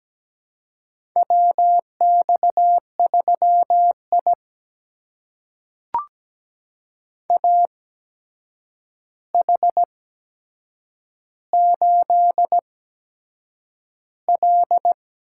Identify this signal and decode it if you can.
Morse code, heard as a keyed tone, about 17 wpm, spaced as for 7 wpm, 700 Hz. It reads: WX3I AH8L
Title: USA Call Signs - 8x Character Spacing - 17wpm